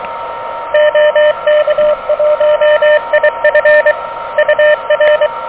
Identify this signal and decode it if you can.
Morse code, heard as a keyed tone, about 24 wpm, 590 Hz, with radio noise. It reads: OK1IF UR
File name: OK QRP 2023 / OK QRP 2023